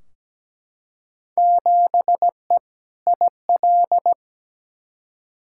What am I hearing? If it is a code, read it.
Morse code, heard as a keyed tone, about 17 wpm, 700 Hz. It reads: 7E IL
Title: Field Day Exchanges Encore - 17wpm